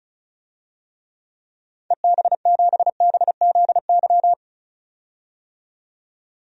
Morse code, heard as a keyed tone, about 34 wpm, 700 Hz. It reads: EB7BZY
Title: DXCC-100 Call Signs - 6 Characters - 35wpm